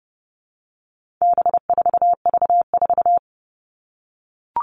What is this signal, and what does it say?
Morse code, heard as a keyed tone, about 30 wpm, 700 Hz. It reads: B4V4
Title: All Cut Numbers - 4-Digit 30wpm